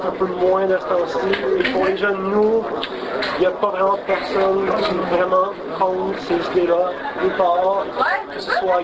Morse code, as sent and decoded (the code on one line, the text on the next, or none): none